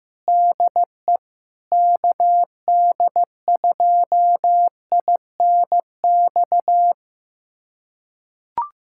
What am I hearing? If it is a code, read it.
Morse code, heard as a keyed tone, about 15 wpm, 700 Hz. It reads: DE KD2INX
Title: USA Callsign DE Callsign 15wpm